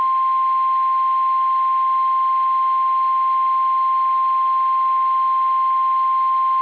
Emergency Alert System (SAME) tones (the 1050 Hz attention tone). It is heard from 0.2 to 5.7 s.